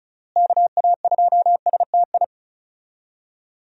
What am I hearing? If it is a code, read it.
Morse code, heard as a keyed tone, about 34 wpm, 700 Hz. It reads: KA2STI
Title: USA Call Signs - 6 Characters - Repeated 2x - 35wpm